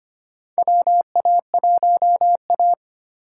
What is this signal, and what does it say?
Morse code, heard as a keyed tone, about 25 wpm, 700 Hz. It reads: WA1A
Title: USA Call Signs - 4 Characters - 4x Character Spacing - 25wpm